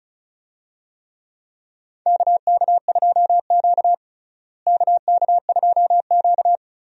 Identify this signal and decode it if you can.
Morse code, heard as a keyed tone, about 36 wpm, 700 Hz. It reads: KK2Q KK2Q
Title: USA Call Signs - 4 Characters - Repeated 3x - 35wpm